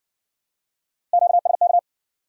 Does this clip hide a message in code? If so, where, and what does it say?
Yes: Morse code from 1.1 to 1.8 s, heard as a keyed tone, about 56 wpm, 700 Hz. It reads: QSK